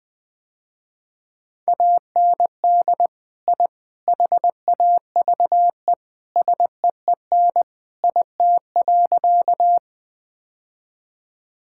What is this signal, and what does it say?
Morse code, heard as a keyed tone, about 20 wpm, 700 Hz. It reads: AND I HAVE SEEN IT.
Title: Sentences with 3-Word Phrases - Spoken Beginning - 20wpm